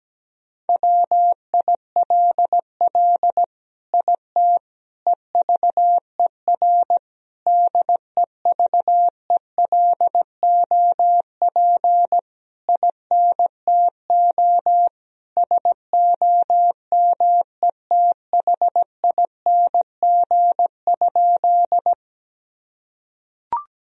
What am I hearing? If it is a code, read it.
Morse code, heard as a keyed tone, about 17 wpm, 700 Hz. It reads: WILL IT EVER DEVELOP INTO SOMETHING?
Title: Sentences from Top 1000 Words - Review Part 2 - 17wpm